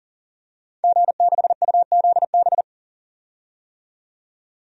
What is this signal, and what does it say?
Morse code, heard as a keyed tone, about 40 wpm, 700 Hz. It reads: G6UZB
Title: DXCC-100 Call Signs - 5 Characters - 40wpm